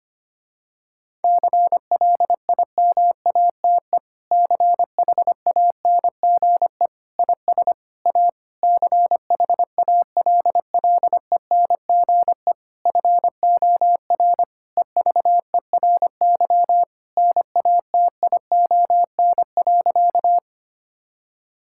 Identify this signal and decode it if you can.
Morse code, heard as a keyed tone, about 25 wpm, 700 Hz. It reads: CLIMATE CHANGE IS A CHALLENGE FOR EVERY NATION.
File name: practice / 25wpm / Difficult Sentences with Hints - 25wpm